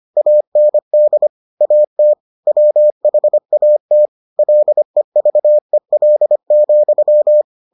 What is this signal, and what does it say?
Morse code, heard as a keyed tone, about 25 wpm, 600 Hz. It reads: AND AT WHAT LEVEL,